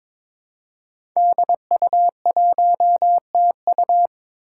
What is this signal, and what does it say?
Morse code, heard as a keyed tone, about 23 wpm, 700 Hz. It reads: DU1TU
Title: Five Challenging Letter-Number Combinations 22wpm